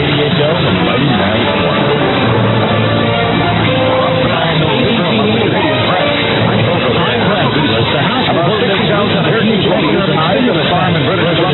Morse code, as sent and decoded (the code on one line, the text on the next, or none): none